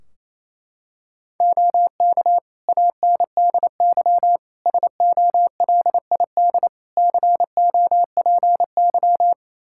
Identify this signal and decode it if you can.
Morse code, heard as a keyed tone, about 28 wpm, 700 Hz. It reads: OK ANDY SOLID COPY
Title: K1USN SST QSO Practice 28wpm